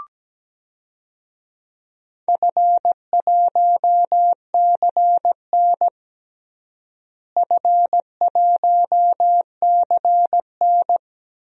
Morse code, heard as a keyed tone, about 17 wpm, 700 Hz. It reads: F1CN F1CN
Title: DXCC-100 Call Signs - 4 Characters - Repeated 3x - 17wpm